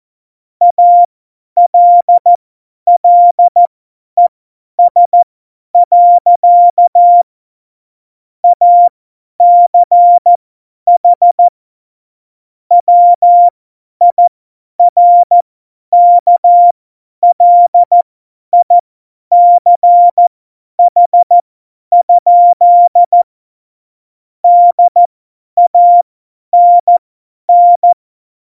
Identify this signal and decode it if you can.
Morse code, heard as a keyed tone, about 14 wpm, spaced as for 10 wpm, 700 Hz. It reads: ALLES. ACH WIRKLICH? DANN
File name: Morsekurs / Witze / Witz 31. Oktober 17, Tempo 50